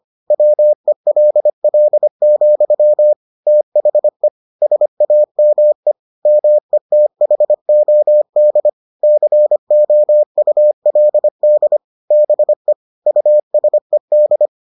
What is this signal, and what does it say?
Morse code, heard as a keyed tone, about 25 wpm, 600 Hz. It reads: WELL, THE SAME METHOD COULD BE USED